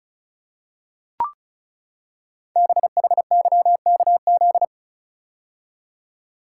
Morse code, heard as a keyed tone, about 36 wpm, 700 Hz. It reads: BHYKZ
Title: Code Groups - 5 Letters - 35wpm